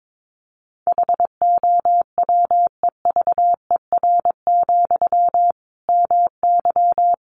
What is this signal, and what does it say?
Morse code, heard as a keyed tone, about 22 wpm, 700 Hz. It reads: HOWEVER, MY
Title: Sentences from Top 1000 Words 22wpm